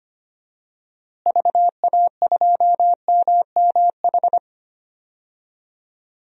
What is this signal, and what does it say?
Morse code, heard as a keyed tone, about 25 wpm, 700 Hz. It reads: VA2MMH
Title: DXCC-100 Call Signs - 6 Characters - Repeated 2x - 25wpm